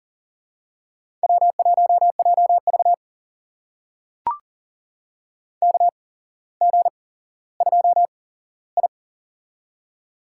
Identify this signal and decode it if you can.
Morse code, heard as a keyed tone, about 39 wpm, spaced as for 17 wpm, 700 Hz. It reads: W1JV KG2I
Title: USA Call Signs - 4 Characters - 8x Character Spacing - 40wpm